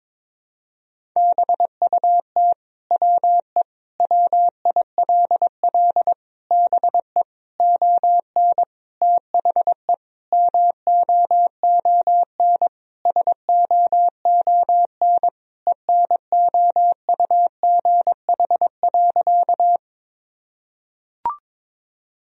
Morse code, heard as a keyed tone, about 22 wpm, 700 Hz. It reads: BUT WE WILL BE ON THE MOON SOON ENOUGH.
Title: Sentences from Top 1000 Words - Review Part 3 - 22wpm